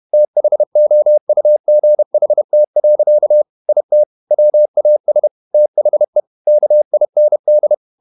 Morse code, heard as a keyed tone, about 31 wpm, 600 Hz. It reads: THOUGHT. IT WAS THE KIND
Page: Never 31wpm | CW med Gnister